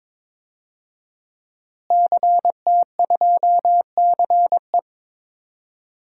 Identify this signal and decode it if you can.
Morse code, heard as a keyed tone, about 21 wpm, 700 Hz. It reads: CT2CE
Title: DXCC-100 Call Signs - 5 Characters - Repeated 3x - 22wpm